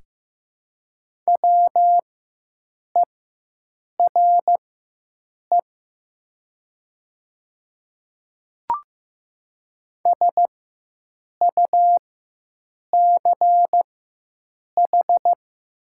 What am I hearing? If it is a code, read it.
Morse code, heard as a keyed tone, about 15 wpm, spaced as for 7 wpm, 700 Hz. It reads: WERE SUCH
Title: 4x Character Spacing - Top 200 Words - 15wpm